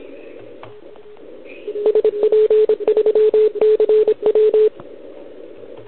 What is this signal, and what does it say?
Morse code, heard as a keyed tone, about 26 wpm, 440 Hz. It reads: SP3CW
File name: ARCHIV / OKDX2020 / OL4Wa